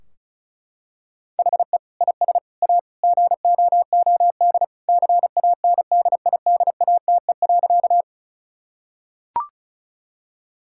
Morse code, heard as a keyed tone, about 36 wpm, 700 Hz. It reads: HE IS A GOOD CANDIDATE.
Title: Sentences from Top 1600 Words 35wpm